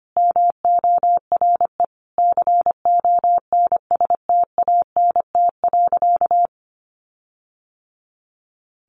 Morse code, heard as a keyed tone, about 25 wpm, 700 Hz. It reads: MORE CONSTANT.
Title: Sentences from Top 1500 Words 25wpm